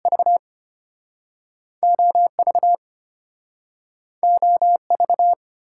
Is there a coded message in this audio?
Yes: Morse code from 0.1 to 5.3 s, heard as a keyed tone, about 34 wpm, 700 Hz. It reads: V OV OV